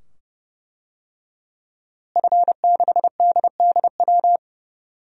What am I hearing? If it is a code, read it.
Morse code, heard as a keyed tone, about 32 wpm, 700 Hz. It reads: F6DDW
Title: DXCC-100 Call Signs - 5 Characters - Repeated 2x - 30wpm